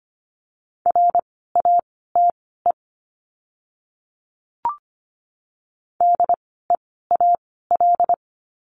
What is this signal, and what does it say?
Morse code, heard as a keyed tone, about 25 wpm, spaced as for 16 wpm, 700 Hz. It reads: RATE DEAL
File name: course / 25:10wpm / 094 Lesson 16f - Letter c - Words 25wpm